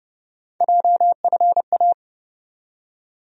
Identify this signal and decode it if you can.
Morse code, heard as a keyed tone, about 30 wpm, 700 Hz. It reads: JFA
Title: USA Call Sign Postfixes - 3 Letters 30wpm